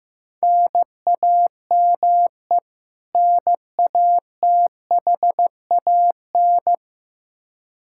Morse code, heard as a keyed tone, about 15 wpm, 700 Hz. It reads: NAME NATHAN
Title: Practice QSO - Exchange 1 - Beginning - 15wpm